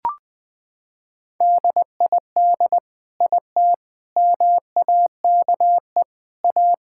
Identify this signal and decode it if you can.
Morse code, heard as a keyed tone, about 20 wpm, 700 Hz. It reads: DID IT MAKE A
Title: Sentences from Top 1000 Words - Review Part 5 - 20wpm